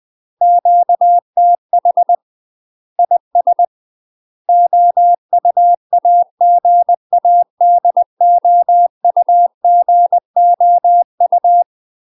Morse code, heard as a keyed tone, about 20 wpm, 700 Hz. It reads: QTH IS OUAGADOUGOU
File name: UK MASTER QSOs TRAINING / 20WPM / UK MASTER QSOs 0034